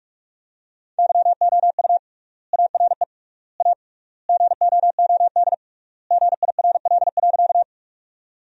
Morse code, heard as a keyed tone, about 48 wpm, 700 Hz. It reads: YOU ARE A GOOD GIRL.
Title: 3x Word-Spacing - Sentences from Top 400 Words - 45wpm